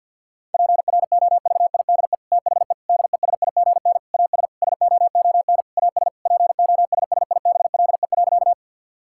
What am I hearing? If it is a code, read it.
Morse code, heard as a keyed tone, about 48 wpm, 700 Hz. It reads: PROVIDE THE DESIGN AS SOON AS POSSIBLE.
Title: Sentences from Top 1000 Words - Review Part 5 - 50wpm